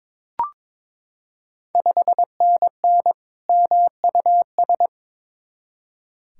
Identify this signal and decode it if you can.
Morse code, heard as a keyed tone, about 22 wpm, 700 Hz. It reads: 5NN MUS
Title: Ontario QSO Party Exchange 22wpm